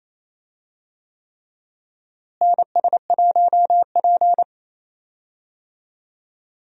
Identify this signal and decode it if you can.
Morse code, heard as a keyed tone, about 28 wpm, 700 Hz. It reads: NS1P
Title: USA Call Signs - 4 Characters - Repeated 3x - 28wpm